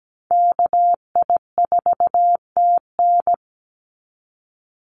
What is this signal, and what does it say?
Morse code, heard as a keyed tone, about 17 wpm, 700 Hz. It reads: KI4TN
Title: NA SOTA Activator Callsigns 17wpm